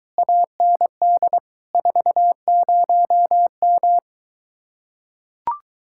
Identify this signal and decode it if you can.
Morse code, heard as a keyed tone, about 23 wpm, 700 Hz. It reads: AND 40M